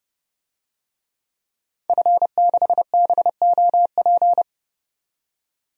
Morse code, heard as a keyed tone, about 32 wpm, 700 Hz. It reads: F6BOP
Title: DXCC-100 Call Signs - 5 Characters - 30wpm